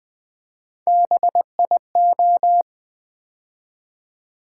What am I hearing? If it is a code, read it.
Morse code, heard as a keyed tone, about 20 wpm, 700 Hz. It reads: BIO
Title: Build A Word - 3 Letters At A Time - 9-Letter Words - 20wpm